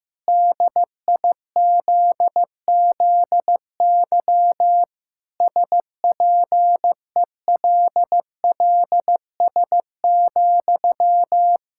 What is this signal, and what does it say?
Morse code, heard as a keyed tone, about 15 wpm, 700 Hz. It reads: DIZZY SPELLS,
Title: Pangram Challenge v2 - 15wpm